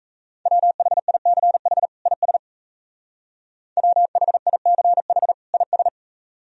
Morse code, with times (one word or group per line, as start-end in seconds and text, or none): 0.5-1.9 WHICH
2.1-2.4 IS
3.8-5.3 WHICH
5.5-5.9 IS